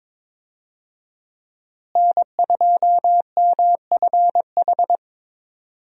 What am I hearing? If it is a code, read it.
Morse code, heard as a keyed tone, about 22 wpm, 700 Hz. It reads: N2MFH
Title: USA Call Signs - 5 Characters - Repeated 2x - 22wpm